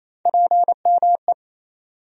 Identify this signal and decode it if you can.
Morse code, heard as a keyed tone, about 28 wpm, 700 Hz. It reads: PME
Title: USA Call Sign Postfixes - 3 Letters 28wpm